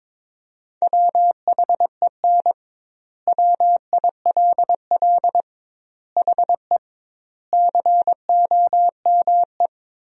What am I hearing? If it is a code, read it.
Morse code, heard as a keyed tone, about 22 wpm, 700 Hz. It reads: WHEN WILL HE COME
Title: 2x Word-Spacing - Sentences from Top 200 Words - 22wpm